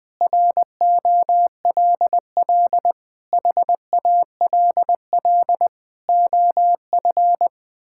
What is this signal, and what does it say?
Morse code, heard as a keyed tone, about 20 wpm, 700 Hz. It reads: ROLL HALL OF
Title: Sentences with 6-Word Phrases - 20wpm